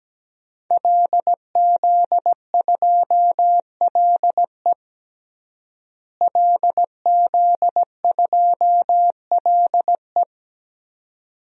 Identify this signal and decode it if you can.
Morse code, heard as a keyed tone, about 17 wpm, 700 Hz. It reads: LZ2LE LZ2LE
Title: DXCC-100 Call Signs - 5 Characters - Repeated 2x - 17wpm